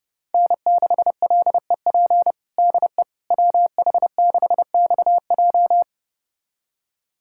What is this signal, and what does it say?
Morse code, heard as a keyed tone, about 32 wpm, 700 Hz. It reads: N6LEP DE WH6XJ
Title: USA Callsign DE Callsign - 5 Characters - 30wpm